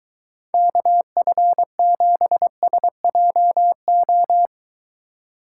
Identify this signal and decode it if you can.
Morse code, heard as a keyed tone, about 22 wpm, 700 Hz. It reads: KF7SJO